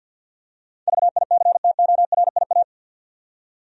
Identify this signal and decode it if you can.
Morse code, heard as a keyed tone, about 52 wpm, 700 Hz. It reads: VICTORIA
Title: USA Common Names - 50wpm